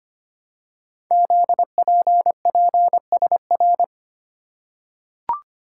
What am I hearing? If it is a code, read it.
Morse code, heard as a keyed tone, about 25 wpm, 700 Hz. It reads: ZPPSR
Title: Code Groups - 5 Letters - 25wpm